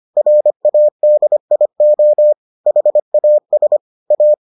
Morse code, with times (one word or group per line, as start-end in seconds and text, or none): 0.2-2.3 RADIO
2.7-3.8 HAS
4.1-4.3 A